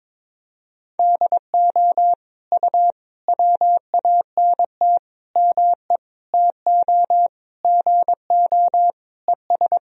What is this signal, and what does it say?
Morse code, heard as a keyed tone, about 22 wpm, 700 Hz. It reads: DO U WANT ME TO GO ES
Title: Sentences from Top 200 Words and Abbreviations 22wpm